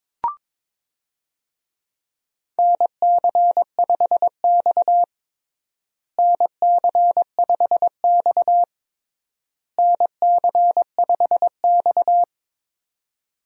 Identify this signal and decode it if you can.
Morse code, heard as a keyed tone, about 22 wpm, 700 Hz. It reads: NC5X NC5X NC5X
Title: USA Call Signs - 4 Characters - Repeated 3x - 22wpm